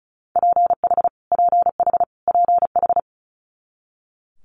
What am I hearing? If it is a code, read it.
Morse code, heard as a keyed tone, about 36 wpm, 700 Hz. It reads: PH PH PH
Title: Multi-letter Phonemes Repeated 3x 35wpm